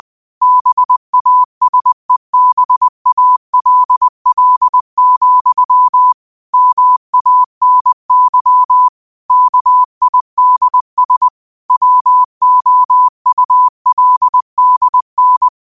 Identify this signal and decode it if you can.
Morse code, heard as a keyed tone, about 20 wpm, 1000 Hz. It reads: BASEBALL, MANY KIDS WOULDN
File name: qotdfiles / 20wpm